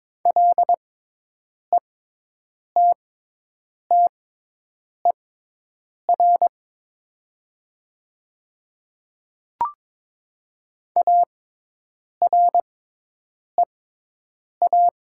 Morse code, heard as a keyed tone, about 22 wpm, spaced as for 8 wpm, 700 Hz. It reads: LETTER AREA
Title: 6x Character Spacing - Top 500 Words - 22wpm